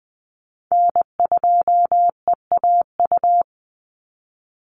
Morse code, heard as a keyed tone, about 20 wpm, 700 Hz. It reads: N2EAU
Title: USA Call Signs 20wpm